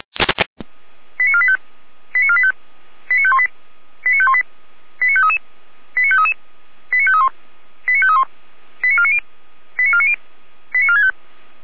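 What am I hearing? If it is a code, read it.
Five-tone selective call (ZVEI), heard as quick runs of five tones, tones of about 70 ms. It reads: 89485 89485 89518 89518 89533 89533 89531 89531 89490 89490 89465
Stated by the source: ZVEI five-tone selective call received by SDR, logged by the Feuerwehr Orlamünde fire brigade.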